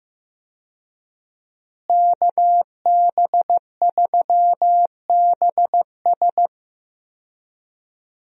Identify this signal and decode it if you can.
Morse code, heard as a keyed tone, about 15 wpm, 700 Hz. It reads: KB3BS
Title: USA Call Signs - 5 Characters - Repeated 3x - 15wpm